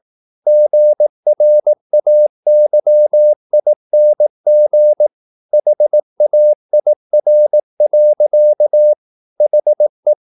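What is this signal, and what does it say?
Morse code, heard as a keyed tone, about 18 wpm, 600 Hz. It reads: GRAYING HAIR. HE
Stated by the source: Morse practice at 18 wpm, English text, from CW med Gnister.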